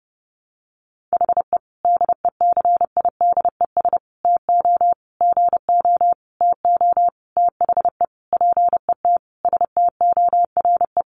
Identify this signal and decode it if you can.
Morse code, heard as a keyed tone, about 30 wpm, 700 Hz. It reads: HE DECIDES TO GO TO THE PET STORE
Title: Additional Easy Phrases 30wpm